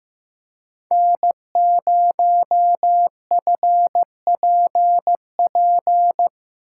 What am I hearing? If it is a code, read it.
Morse code, heard as a keyed tone, about 15 wpm, 700 Hz. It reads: N0FPP
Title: USA Call Signs - 5 Characters - 8x Character Spacing - 15wpm